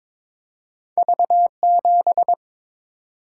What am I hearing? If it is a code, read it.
Morse code, heard as a keyed tone, about 22 wpm, 700 Hz. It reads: V7
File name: practice / 22wpm / Two Letter-Number Combinations - 22wpm